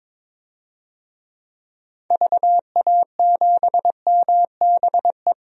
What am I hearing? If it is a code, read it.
Morse code, heard as a keyed tone, about 22 wpm, 700 Hz. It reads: VA7MBE
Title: DXCC-100 Call Signs - 6 Characters - Repeated 2x - 22wpm